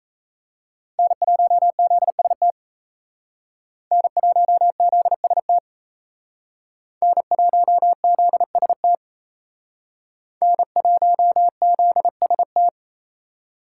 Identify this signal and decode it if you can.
Morse code, heard as a keyed tone, about 36 wpm, 700 Hz. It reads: N1ZST N1ZST N1ZST N1ZST